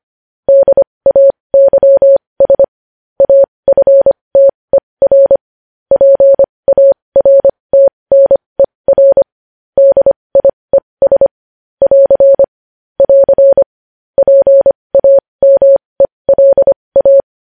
Morse code, heard as a keyed tone, about 25 wpm, spaced as for 20 wpm, 550 Hz. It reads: DAYS AFTER PARTNER DIES <AR> <AR> PAMELA
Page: news-headlines-in-morse-code-at-20-wpm